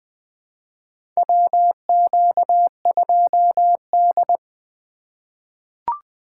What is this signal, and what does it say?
Morse code, heard as a keyed tone, about 21 wpm, 700 Hz. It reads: WQ2D
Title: USA Call Signs - 4 Characters - 4x Character Spacing - 20wpm